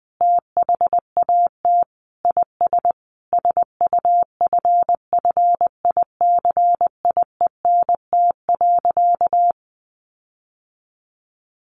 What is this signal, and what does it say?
Morse code, heard as a keyed tone, about 19 wpm, 700 Hz. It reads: THAT IS SUFFICIENT.
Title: Sentences from Top 1400 Words 20wpm